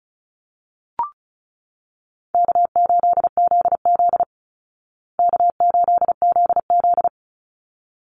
Morse code, heard as a keyed tone, about 34 wpm, 700 Hz. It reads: K8ZZ K8ZZ
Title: Call Signs Repeated 3x - 35wpm